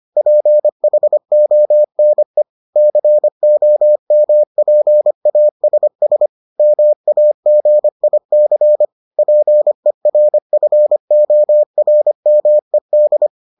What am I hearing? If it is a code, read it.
Morse code, heard as a keyed tone, about 25 wpm, 600 Hz. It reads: PHONE COMPASS MAGIC PERFORMED